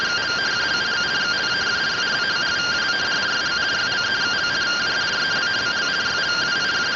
Two-tone teleprinter signal, about 170 Hz shift.